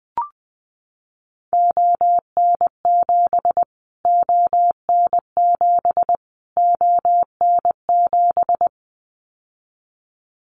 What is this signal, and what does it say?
Morse code, heard as a keyed tone, about 20 wpm, 700 Hz. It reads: ON7 ON7 ON7
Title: Callsign Prefixes for Top 25 DXCC - Repeated 3x - 20wpm